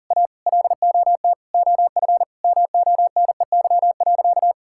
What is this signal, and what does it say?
Morse code, heard as a keyed tone, about 40 wpm, 700 Hz. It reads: A LOT OF MONEY.
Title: Sentences from Top 1000 Words - Review Part 6 - 40wpm